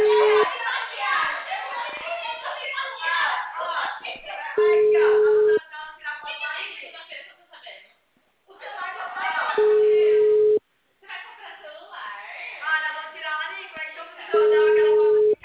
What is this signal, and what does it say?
Ringback tone (Europe, 425 Hz), with speech.